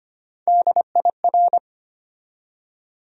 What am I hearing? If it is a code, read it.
Morse code, heard as a keyed tone, about 25 wpm, 700 Hz. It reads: DIR
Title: USA Call Sign Postfixes - 3 Letters 25wpm